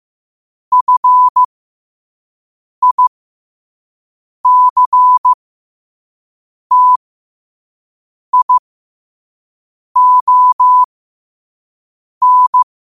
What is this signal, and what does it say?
Morse code, heard as a keyed tone, about 15 wpm, spaced as for 5 wpm, 1000 Hz. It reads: FICTION
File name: qotdfiles / 5wpm